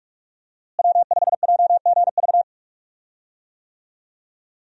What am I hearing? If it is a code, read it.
Morse code, heard as a keyed tone, about 46 wpm, 700 Hz. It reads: W5JGV
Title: USA Call Signs - 5 Characters - Repeated 2x - 45wpm